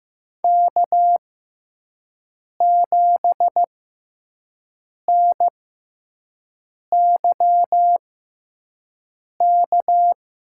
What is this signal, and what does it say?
Morse code, heard as a keyed tone, about 15 wpm, spaced as for 5 wpm, 700 Hz. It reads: K7NYK
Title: USA Call Signs - 5 Characters - 6x Character Spacing - 15wpm